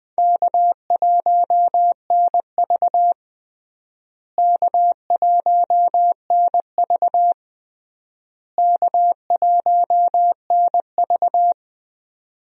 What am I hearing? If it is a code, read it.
Morse code, heard as a keyed tone, about 20 wpm, 700 Hz. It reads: K1NV K1NV K1NV
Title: USA Call Signs - 4 Characters - Repeated 3x - 20wpm